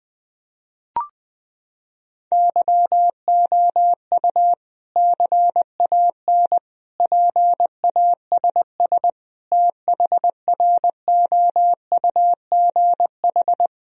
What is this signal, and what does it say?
Morse code, heard as a keyed tone, about 21 wpm, 700 Hz. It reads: YOU CAN PASS THROUGH
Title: Sentences from Top 1000 Words - Review Part 1 - 20wpm